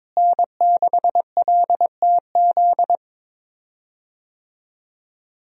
Morse code, heard as a keyed tone, about 22 wpm, 700 Hz. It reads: N6LTZ
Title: USA Call Signs - 5 Characters - Repeated 3x - 22wpm